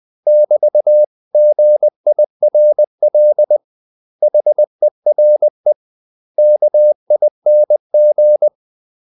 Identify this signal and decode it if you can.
Morse code, heard as a keyed tone, about 20 wpm, 600 Hz. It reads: <BT>GIRL HERE KING
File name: CW / Practice / 500MostCommonEnglishWords20